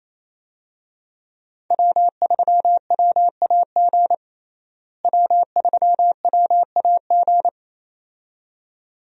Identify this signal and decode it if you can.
Morse code, heard as a keyed tone, about 28 wpm, 700 Hz. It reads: W3WAG W3WAG
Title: USA Call Signs - 5 Characters - Repeated 2x - 28wpm